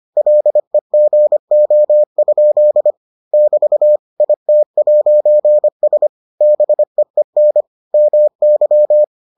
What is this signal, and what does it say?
Morse code, heard as a keyed tone, about 25 wpm, 600 Hz. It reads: LEGO? <BT> IT'S BEEN MY